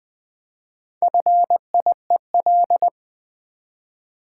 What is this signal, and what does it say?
Morse code, heard as a keyed tone, about 20 wpm, 700 Hz. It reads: FIEL
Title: Build A Word - 1 Letter At A Time - 5-Letter Words - 20wpm